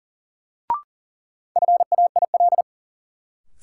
Morse code, heard as a keyed tone, about 40 wpm, 700 Hz. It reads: FAIL
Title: Sentences from Top 1100 Words - New Word Drill - 40wpm